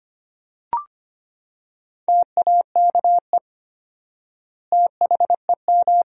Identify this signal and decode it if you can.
Morse code, heard as a keyed tone, about 25 wpm, 700 Hz. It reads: TAKE THEM
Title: 4x Word-Spacing - Sentences from Top 200 Words - 25wpm